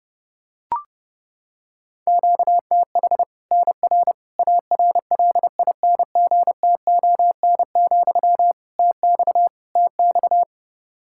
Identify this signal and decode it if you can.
Morse code, heard as a keyed tone, about 30 wpm, 700 Hz. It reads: QTH NR ARLINGTON, TX TX
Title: K1USN SST QSO Practice 30wpm